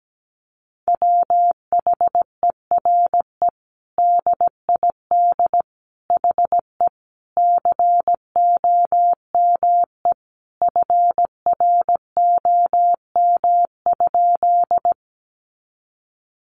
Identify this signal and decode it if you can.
Morse code, heard as a keyed tone, about 17 wpm, 700 Hz. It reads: WHERE DID HE COME FROM?
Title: Sentences from Top 200 Words 17wpm